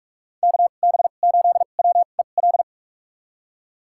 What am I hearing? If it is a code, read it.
Morse code, heard as a keyed tone, about 42 wpm, spaced as for 31 wpm, 700 Hz. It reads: KB8WEL
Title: USA Call Signs - 2x Character Spacing - 45wpm